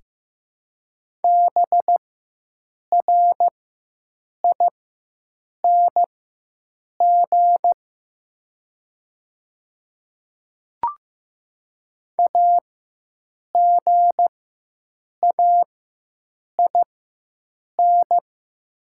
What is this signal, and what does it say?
Morse code, heard as a keyed tone, about 15 wpm, spaced as for 7 wpm, 700 Hz. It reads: BRING AGAIN
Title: 4x Character Spacing - Top 500 Words - 15wpm